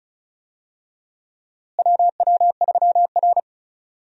Morse code, heard as a keyed tone, about 36 wpm, 700 Hz. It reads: WW3R
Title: USA Call Signs - 4 Characters - Repeated 2x - 35wpm